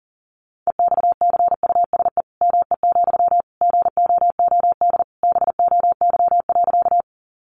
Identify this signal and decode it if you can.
Morse code, heard as a keyed tone, about 40 wpm, 700 Hz. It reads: EXCUSE ME, GOOD BOY.
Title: Sentences from Top 700 Words 40wpm